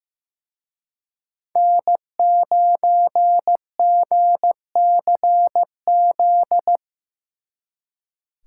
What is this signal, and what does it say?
Morse code, heard as a keyed tone, about 18 wpm, 700 Hz. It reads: N9GCZ
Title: Speed Racing - USA 1x3 Call Signs 15wpm